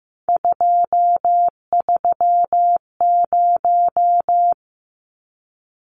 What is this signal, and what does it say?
Morse code, heard as a keyed tone, about 15 wpm, 700 Hz. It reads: 230
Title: Numbers - 3-Digit 15wpm